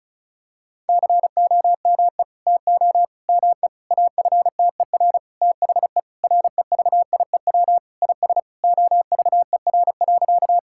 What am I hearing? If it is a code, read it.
Morse code, heard as a keyed tone, about 36 wpm, 700 Hz. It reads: COME TO ME AFTER THE REVIEW IS OVER.
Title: Sentences from Top 1000 Words - Review Part 6 - 35wpm